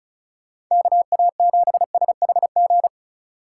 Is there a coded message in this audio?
Yes: Morse code from 0.7 to 2.9 s, heard as a keyed tone, about 44 wpm, 700 Hz. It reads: KA7SHG